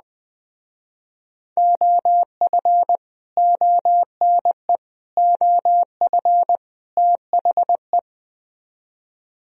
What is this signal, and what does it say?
Morse code, heard as a keyed tone, about 21 wpm, 700 Hz. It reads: OF ONE OF THE
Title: Speed-Racing - Sets of 4 Words 20wpm